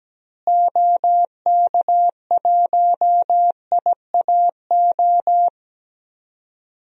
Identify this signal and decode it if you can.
Morse code, heard as a keyed tone, about 17 wpm, 700 Hz. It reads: OK1IAO
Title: DXCC-100 Call Signs - 6 Characters - Repeated 3x - 17wpm